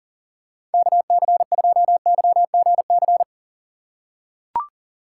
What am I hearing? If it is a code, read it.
Morse code, heard as a keyed tone, about 37 wpm, 700 Hz. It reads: KC2YGC
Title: USA Call Signs - 6 Characters - 3x Character Spacing - 40wpm